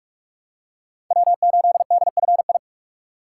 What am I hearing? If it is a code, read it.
Morse code, heard as a keyed tone, about 47 wpm, 700 Hz. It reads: W8DFI
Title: USA Call Signs - 4x Character Spacing - 45wpm